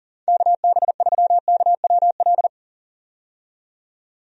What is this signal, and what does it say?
Morse code, heard as a keyed tone, about 40 wpm, 700 Hz. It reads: KB3KWL
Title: USA Call Signs - 6 Characters - 40wpm